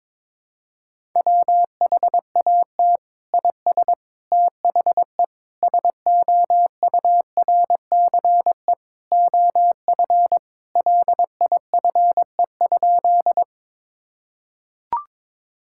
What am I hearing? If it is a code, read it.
Morse code, heard as a keyed tone, about 22 wpm, 700 Hz. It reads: WHAT IS THE SOURCE OF LIFE?
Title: Sentences from Top 1000 Words - Review Part 5 - 22wpm